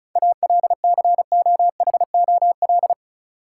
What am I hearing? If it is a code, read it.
Morse code, heard as a keyed tone, about 34 wpm, 700 Hz. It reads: ALCOHOL
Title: Sentences from Top 1400 Words - New Word Drill - 35wpm